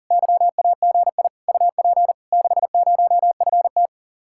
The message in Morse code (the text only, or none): YAGI UP 60FT